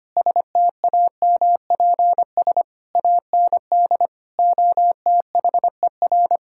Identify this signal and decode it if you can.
Morse code, heard as a keyed tone, about 25 wpm, 700 Hz. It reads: STAMPS AND OTHER
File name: practice / 25wpm / Sentences with 2-Word Phrases - 25wpm